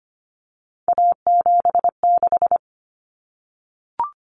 Morse code, heard as a keyed tone, about 25 wpm, 700 Hz. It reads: A76
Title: Common Cut Numbers - 3-Digit 25wpm